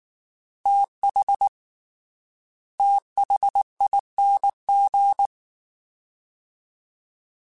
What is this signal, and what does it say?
Morse code, heard as a keyed tone, about 19 wpm, 800 Hz. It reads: TH THING
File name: TH / TH mp3s / TH Prefixes 3x2 Twice @18